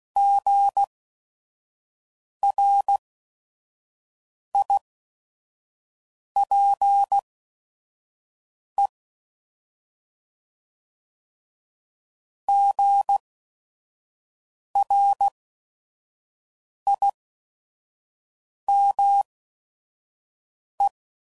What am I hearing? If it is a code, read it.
Morse code, heard as a keyed tone, about 16 wpm, spaced as for 5 wpm, 800 Hz. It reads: GRIPE GRIME
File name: Prefixes / By First Letter / Gs